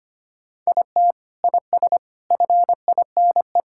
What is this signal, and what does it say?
Morse code, heard as a keyed tone, about 25 wpm, 700 Hz. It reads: IT IS FINE
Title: Sentences from Top 1000 Words - Review Part 5 - 25wpm